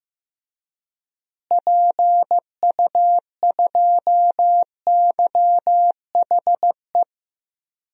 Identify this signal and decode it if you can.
Morse code, heard as a keyed tone, about 15 wpm, 700 Hz. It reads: PU2YHE
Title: DXCC-100 Call Signs - 6 Characters - Repeated 2x - 15wpm